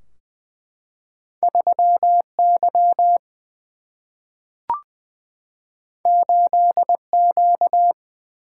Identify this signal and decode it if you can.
Morse code, heard as a keyed tone, about 20 wpm, 700 Hz. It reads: 3Y 8Q
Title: Single-Letter Single-Number 20wpm